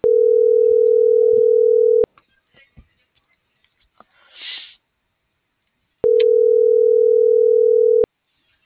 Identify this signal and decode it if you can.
Ringback tone (North America), 440 + 480 Hz.